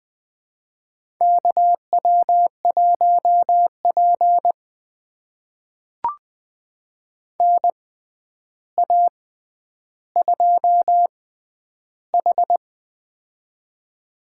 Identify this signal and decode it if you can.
Morse code, heard as a keyed tone, about 20 wpm, spaced as for 10 wpm, 700 Hz. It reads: KW1P NA2H
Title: USA Call Signs - 6x Character Spacing - 20wpm